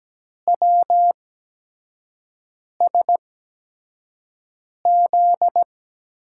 Morse code, heard as a keyed tone, about 17 wpm, spaced as for 5 wpm, 700 Hz. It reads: WSZ
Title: USA Call Signs - 8x Character Spacing - 17wpm